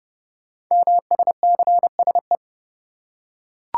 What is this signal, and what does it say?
Morse code, heard as a keyed tone, about 30 wpm, 700 Hz. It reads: MSCSE
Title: Code Groups - 5 Letter:Numbers - 30wpm